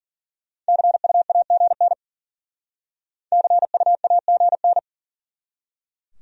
Morse code, heard as a keyed tone, about 50 wpm, 700 Hz. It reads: CUAGN CUAGN
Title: Speed-Racing Abbreviations - 40wpm